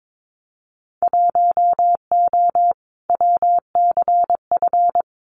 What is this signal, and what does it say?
Morse code, heard as a keyed tone, about 22 wpm, 700 Hz. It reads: 1O WCF
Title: Winter Field Day Exchanges 22wpm